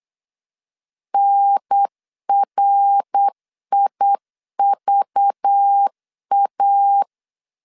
Morse code, heard as a keyed tone, about 8 wpm, 800 Hz. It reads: NRIVA
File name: LEZIONI / 9 2 8wpm 3437